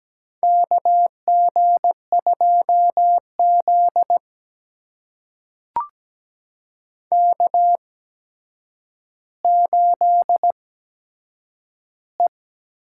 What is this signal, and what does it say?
Morse code, heard as a keyed tone, about 17 wpm, spaced as for 8 wpm, 700 Hz. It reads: KG2Z K8E
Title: USA Call Signs - 4 Characters - 8x Character Spacing - 17wpm